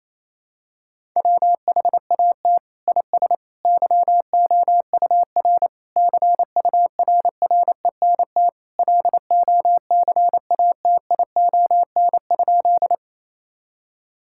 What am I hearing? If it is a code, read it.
Morse code, heard as a keyed tone, about 28 wpm, 700 Hz. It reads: WHAT IS YOUR CURRENT LOCATION?
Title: Sentences from Top 1000 Words - Review Part 6 - 28wpm